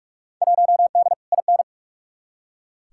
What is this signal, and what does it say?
Morse code, heard as a keyed tone, about 48 wpm, 700 Hz. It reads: 1D IN
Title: Field Day Exchanges - 45wpm